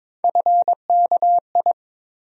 Morse code, heard as a keyed tone, about 22 wpm, 700 Hz. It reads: FKI
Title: USA Call Sign Postfixes - 3 Letters 22wpm